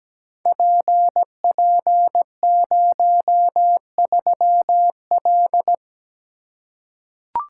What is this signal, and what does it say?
Morse code, heard as a keyed tone, about 17 wpm, 700 Hz. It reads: PP03L
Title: Code Groups - 5 Letter:Numbers - 17wpm